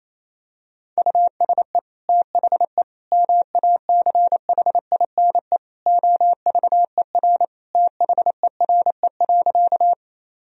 Morse code, heard as a keyed tone, about 28 wpm, 700 Hz. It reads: USE THE MACHINE OVER THERE.
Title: Sentences from Top 1000 Words - Review Part 2 - 28wpm